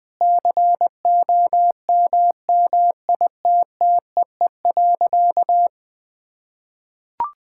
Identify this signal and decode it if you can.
Morse code, heard as a keyed tone, about 20 wpm, 700 Hz. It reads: COMMITTEE.
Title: Sentences with 6-Word Phrases - 20wpm